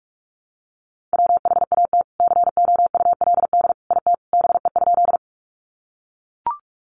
Morse code, heard as a keyed tone, about 44 wpm, 700 Hz. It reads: WHAT COULD IT BE?
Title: Sentences from Top 100 Words 45wpm